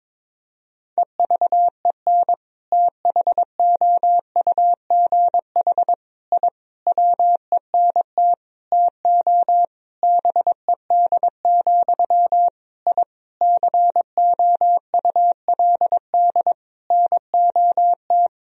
Morse code, heard as a keyed tone, about 22 wpm, 700 Hz. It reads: EVEN THOUGH I WENT TO BED, I COULD NOT
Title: Sentences from Top 1000 Words - Review Part 2 - 22wpm